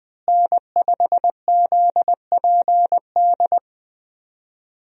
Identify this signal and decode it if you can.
Morse code, heard as a keyed tone, about 20 wpm, 700 Hz. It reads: N5ZPD
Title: USA Call Signs - 5 Characters - Repeated 2x - 20wpm